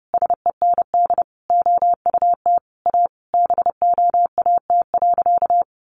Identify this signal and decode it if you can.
Morse code, heard as a keyed tone, about 30 wpm, 700 Hz. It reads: SEND OUT A BOAT.
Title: Sentences from Top 1500 Words 30wpm